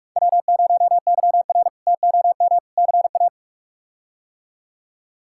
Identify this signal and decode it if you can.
Morse code, heard as a keyed tone, about 45 wpm, 700 Hz. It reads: W0YR TOM CA
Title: K1USN SST QSO Practice - 45wpm